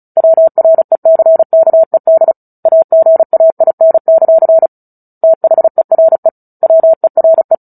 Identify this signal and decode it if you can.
Morse code, heard as a keyed tone, about 34 wpm, 650 Hz. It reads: WRECKED AGAIN; THERE WERE
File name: Book Two / 09-Chapter 9 - 35 WPM